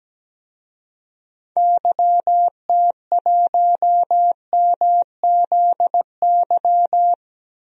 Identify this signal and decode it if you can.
Morse code, heard as a keyed tone, about 17 wpm, 700 Hz. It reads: YT1MZY
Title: DXCC-100 Call Signs - 6 Characters - Repeated 2x - 17wpm